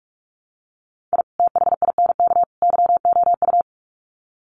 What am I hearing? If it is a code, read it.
Morse code, heard as a keyed tone, about 44 wpm, 700 Hz. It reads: I THINK YOU
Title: Sets of 3 Words 45wpm